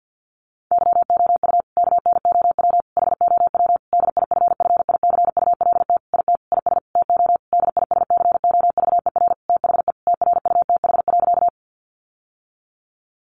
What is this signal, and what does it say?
Morse code, heard as a keyed tone, about 49 wpm, 700 Hz. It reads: YOU KNOW HOW DIFFICULT IT IS TO DISCOVER THE TRUTH.
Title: Sentences from Top 1100 Words 50wpm